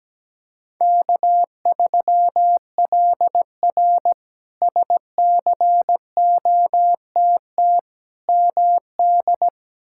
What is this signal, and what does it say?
Morse code, heard as a keyed tone, about 17 wpm, 700 Hz. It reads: K3LR SCOTT MD
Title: K1USN SST QSO Practice - 17wpm